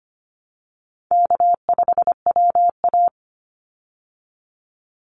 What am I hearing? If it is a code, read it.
Morse code, heard as a keyed tone, about 25 wpm, 700 Hz. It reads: K5WA
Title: Easy Call Signs Repeated 3x - 25wpm